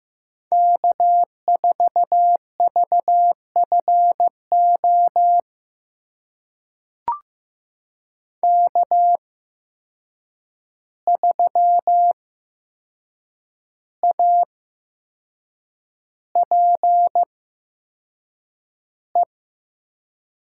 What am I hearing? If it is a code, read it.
Morse code, heard as a keyed tone, about 15 wpm, spaced as for 6 wpm, 700 Hz. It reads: K4VFO K3APE
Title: USA Call Signs - 5 Characters - 8x Character Spacing - 15wpm